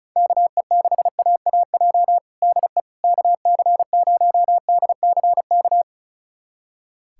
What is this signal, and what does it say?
Morse code, heard as a keyed tone, about 36 wpm, 700 Hz. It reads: KE6AAJ DE KC0DCK
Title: USA Callsign DE Callsign - 6 Characters - 35wpm